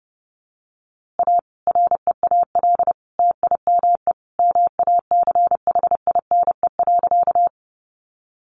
Morse code, heard as a keyed tone, about 29 wpm, 700 Hz. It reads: A REAL TIME MACHINE.
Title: A Sound Of Thunder 30wpm